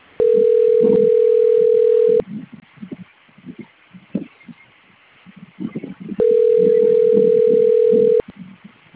Ringback tone (North America), 440 + 480 Hz.